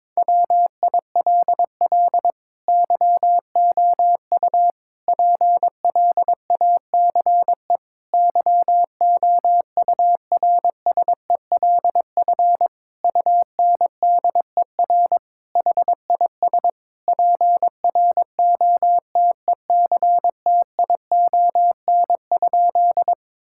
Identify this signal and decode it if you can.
Morse code, heard as a keyed tone, about 22 wpm, 700 Hz. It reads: WILL YOU PLACE YOURSELF UNDER HIS PROTECTION?
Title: Sentences from Top 1600 Words 22wpm